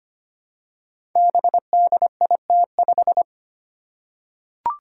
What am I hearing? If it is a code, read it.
Morse code, heard as a keyed tone, about 25 wpm, 700 Hz. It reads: BDIT5
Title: Five Challenging Letter-Number Combinations 25wpm